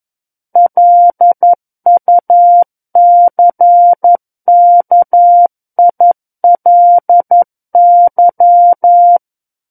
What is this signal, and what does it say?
Morse code, heard as a keyed tone, about 11 wpm, 700 Hz. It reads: LUCKILY
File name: 11 WPM